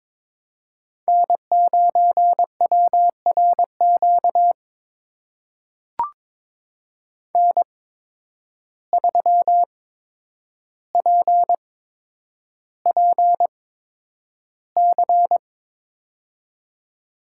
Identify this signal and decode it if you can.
Morse code, heard as a keyed tone, about 22 wpm, spaced as for 9 wpm, 700 Hz. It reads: N9WRQ N3PPC
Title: USA Call Signs - 5 Characters - 8x Character Spacing - 22wpm